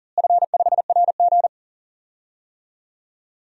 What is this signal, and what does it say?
Morse code, heard as a keyed tone, about 40 wpm, 700 Hz. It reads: F5RG
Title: DXCC-100 Call Signs - 4 Characters - Repeated 3x - 40wpm